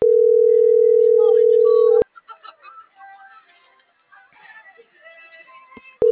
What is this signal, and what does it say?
Ringback tone (North America), 440 + 480 Hz.